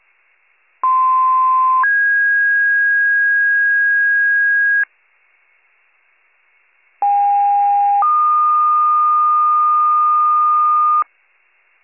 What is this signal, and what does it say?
Two-tone page, heard as a one-second tone, then a three-second tone, about 1040 then 1650 Hz, about 810 then 1220 Hz. It is heard from 0.8 to 11.0 s.